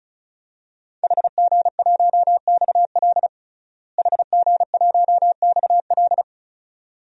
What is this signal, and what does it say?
Morse code, heard as a keyed tone, about 34 wpm, 700 Hz. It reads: HG1XL HG1XL
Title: DXCC-100 Call Signs - 5 Characters - Repeated 2x - 35wpm